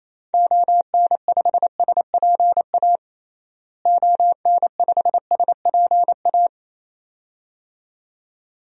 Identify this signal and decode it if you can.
Morse code, heard as a keyed tone, about 28 wpm, 700 Hz. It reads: ON5SPA ON5SPA
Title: DXCC-100 Call Signs - 6 Characters - Repeated 2x - 28wpm